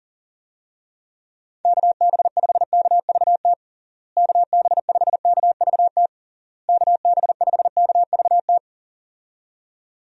Morse code, heard as a keyed tone, about 40 wpm, 700 Hz. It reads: KB5KVT KB5KVT KB5KVT
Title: USA Call Signs - 6 Characters - Repeated 3x - 40wpm